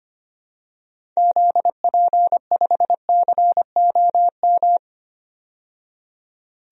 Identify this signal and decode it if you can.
Morse code, heard as a keyed tone, about 25 wpm, 700 Hz. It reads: ZP5COM
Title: DXCC-100 Call Signs - 6 Characters - 25wpm